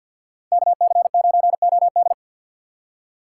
Morse code, heard as a keyed tone, about 50 wpm, 700 Hz. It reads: KC9OD
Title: USA Call Signs - 5 Characters - 2x Character Spacing - 50wpm